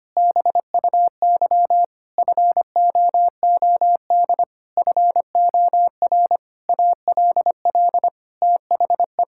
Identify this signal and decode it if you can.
Morse code, heard as a keyed tone, about 25 wpm, 700 Hz. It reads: BUY FOOD FOR ALL THE
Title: Sentences from Top 1600 Words 25wpm